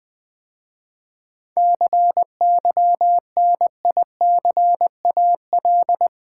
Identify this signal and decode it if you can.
Morse code, heard as a keyed tone, about 20 wpm, 700 Hz. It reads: CYNICAL
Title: 7 Letter Words - First 3 Letters Given - 20wpm